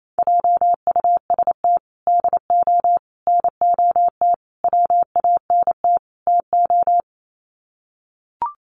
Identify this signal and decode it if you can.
Morse code, heard as a keyed tone, about 28 wpm, 700 Hz. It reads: JUST DO NOT WANT TO
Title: Sets of 5 Words Encore 28wpm